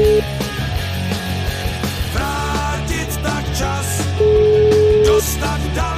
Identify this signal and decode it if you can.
Ringback tone (Europe, 425 Hz), with noise.